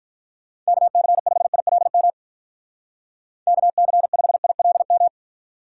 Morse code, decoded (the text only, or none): KC5ILM KC5ILM